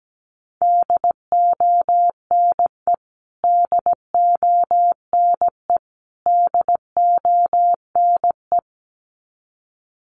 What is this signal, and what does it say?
Morse code, heard as a keyed tone, about 17 wpm, 700 Hz. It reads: DONE DONE DONE
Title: Top 500 Words Repeated 17wpm